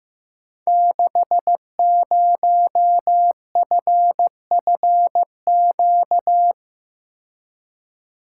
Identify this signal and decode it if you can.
Morse code, heard as a keyed tone, about 15 wpm, 700 Hz. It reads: 60FFQ
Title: Code Groups - 5 Letter:Numbers - 15wpm